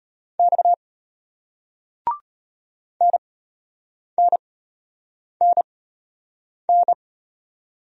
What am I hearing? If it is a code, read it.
Morse code, heard as a keyed tone, about 34 wpm, spaced as for 8 wpm, 700 Hz. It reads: X NNNN